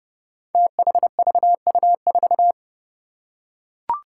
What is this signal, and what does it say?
Morse code, heard as a keyed tone, about 30 wpm, 700 Hz. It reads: THVU4
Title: Five Challenging Letter-Number Combinations 30wpm